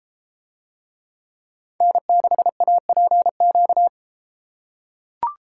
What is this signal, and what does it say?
Morse code, heard as a keyed tone, about 31 wpm, 700 Hz. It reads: N6APQ